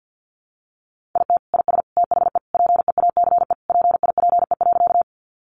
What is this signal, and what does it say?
Morse code, heard as a keyed tone, about 50 wpm, 700 Hz. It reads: IT IS THE PEACE PIPE.
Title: Sentences from Top 1500 Words 50wpm